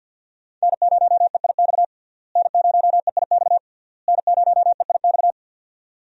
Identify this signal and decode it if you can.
Morse code, heard as a keyed tone, about 51 wpm, 700 Hz. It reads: N0EIX N0EIX N0EIX
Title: USA Call Signs - 5 Characters - Repeated 3x - 50wpm